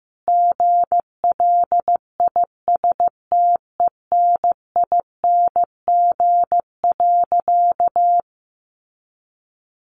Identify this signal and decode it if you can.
Morse code, heard as a keyed tone, about 14 wpm, 700 Hz. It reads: GLISTENING.
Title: A Sound Of Thunder 15wpm